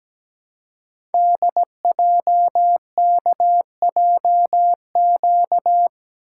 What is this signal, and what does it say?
Morse code, heard as a keyed tone, about 17 wpm, 700 Hz. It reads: DJKJQ
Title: Code Groups - 5 Letters - 17wpm